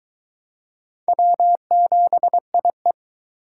Morse code, heard as a keyed tone, about 22 wpm, 700 Hz. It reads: W7IE